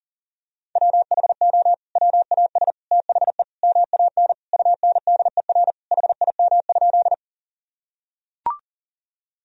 Morse code, heard as a keyed tone, about 40 wpm, 700 Hz. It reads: WHO WAS THE MAN UNDER HIM?
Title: Sentences from Top 1000 Words - Review Part 1 - 40wpm